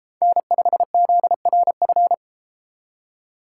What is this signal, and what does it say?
Morse code, heard as a keyed tone, about 33 wpm, 700 Hz. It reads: N5ZRF